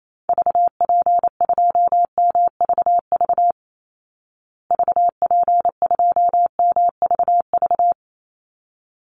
Morse code, heard as a keyed tone, about 28 wpm, 700 Hz. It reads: VP2MVV VP2MVV
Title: Call Signs Repeated 3x - 28wpm